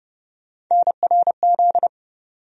Morse code, heard as a keyed tone, about 31 wpm, 700 Hz. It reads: NRZ
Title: USA Call Sign Postfixes - 3 Letters 30wpm